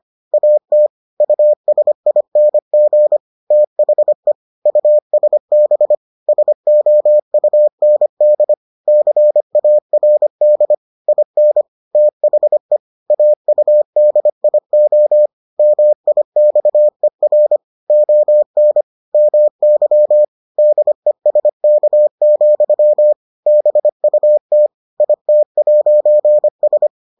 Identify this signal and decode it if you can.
Morse code, heard as a keyed tone, about 25 wpm, 600 Hz. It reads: AT USING THE USB SOUND CARD IN THE AUDIO MIXER ON MY DESK, BUT IT'S